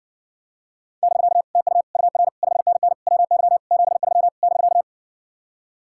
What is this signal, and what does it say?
Morse code, heard as a keyed tone, about 60 wpm, 700 Hz. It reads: <BK> TU UR 5NN WY 73 <BK>